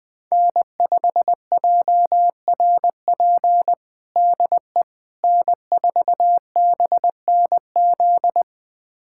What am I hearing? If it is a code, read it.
Morse code, heard as a keyed tone, about 20 wpm, 700 Hz. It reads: N5JRP DE N4BNZ
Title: USA Callsign DE Callsign - 5 Characters - 20wpm